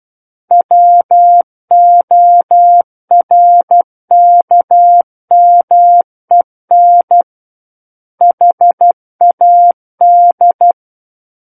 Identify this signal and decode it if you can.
Morse code, heard as a keyed tone, about 12 wpm, 700 Hz. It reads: WORKMEN HAD
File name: Book One / 03-Chapter 3 - 12 WPM